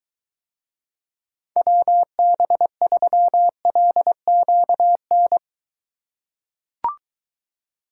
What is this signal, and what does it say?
Morse code, heard as a keyed tone, about 23 wpm, 700 Hz. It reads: WB3LQN